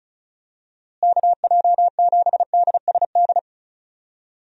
Morse code, heard as a keyed tone, about 34 wpm, 700 Hz. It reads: KJ7DSD
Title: USA Call Signs - 6 Characters - Repeated 3x - 35wpm